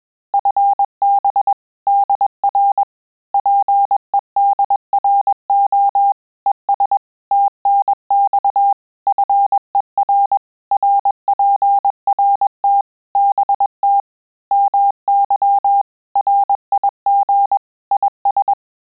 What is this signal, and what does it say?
Morse code, heard as a keyed tone, about 21 wpm, 800 Hz. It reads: FB DR PEDRO ES TNX FER RPRT BT MY RIG IS
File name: QSO CW 100 CPM